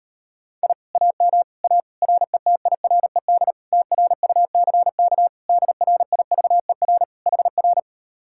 Morse code, heard as a keyed tone, about 39 wpm, 700 Hz. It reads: I AM A RETIRED TRUCK DRIVER HR